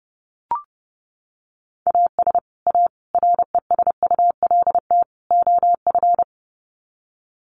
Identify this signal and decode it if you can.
Morse code, heard as a keyed tone, about 30 wpm, 700 Hz. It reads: AS A RESULT OF
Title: Sets of 4 Words 30wpm